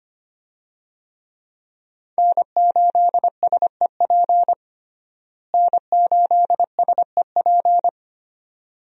Morse code, heard as a keyed tone, about 25 wpm, 700 Hz. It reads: N8SEP N8SEP
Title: USA Call Signs - 5 Characters - Repeated 3x - 25wpm